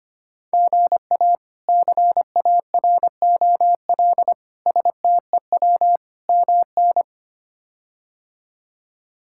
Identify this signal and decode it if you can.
Morse code, heard as a keyed tone, about 25 wpm, 700 Hz. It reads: GA CAROL STEW MN
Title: K1USN SST QSO Practice - 25wpm